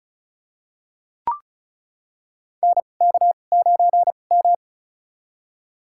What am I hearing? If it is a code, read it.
Morse code, heard as a keyed tone, about 36 wpm, spaced as for 25 wpm, 700 Hz. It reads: NK9M
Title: USA Call Signs - 4 Characters - 2x Character Spacing - 35wpm